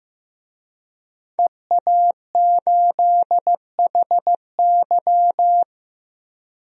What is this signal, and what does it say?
Morse code, heard as a keyed tone, about 15 wpm, 700 Hz. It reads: EA8HY
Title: DXCC-100 Call Signs - 5 Characters - 15wpm